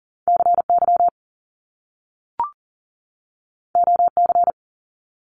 Morse code, heard as a keyed tone, about 39 wpm, 700 Hz. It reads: CY OC
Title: Top Two Letter Combinations in Words 40wpm